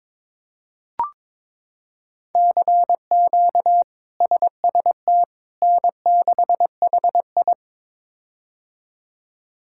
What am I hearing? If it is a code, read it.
Morse code, heard as a keyed tone, about 22 wpm, 700 Hz. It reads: CQ SST N6HI
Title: K1USN SST QSO Practice - 22wpm